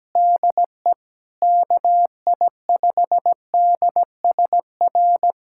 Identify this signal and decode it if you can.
Morse code, heard as a keyed tone, about 17 wpm, 700 Hz. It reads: DE KI5DSR
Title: USA Callsign DE Callsign 17wpm